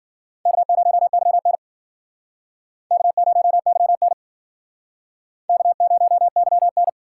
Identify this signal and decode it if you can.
Morse code, heard as a keyed tone, about 52 wpm, 700 Hz. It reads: K0YN K0YN K0YN